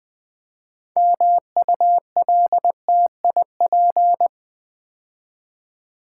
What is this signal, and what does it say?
Morse code, heard as a keyed tone, about 20 wpm, 700 Hz. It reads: MULTIP
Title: Build A Word - 3 Letters At A Time - 8-Letter Words - 20wpm